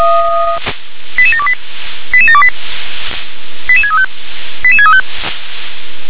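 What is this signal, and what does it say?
Five-tone selective call (ZVEI), heard as quick runs of five tones, tones of about 80 ms. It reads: 88728 88728 88635 88635
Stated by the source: ZVEI five-tone selective call received on SDR.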